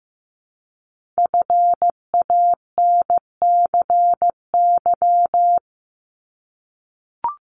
Morse code, heard as a keyed tone, about 15 wpm, 700 Hz. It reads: FANCY
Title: Multi-letter Phonemes 15wpm